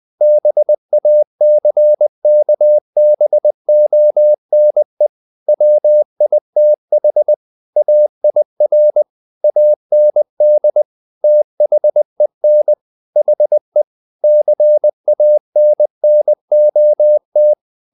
Morse code, heard as a keyed tone, about 20 wpm, 600 Hz. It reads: BACKBONE WITH AIR AND THEN HE CANNOT
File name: the old man and the sea / the old man and the sea part 1 20wpm